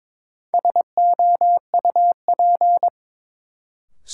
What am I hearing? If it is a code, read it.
Morse code, heard as a keyed tone, about 22 wpm, 700 Hz. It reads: SOUP
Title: Sentences from Top 1000 Words - Review Part 6 - New Word Drill - 22wpm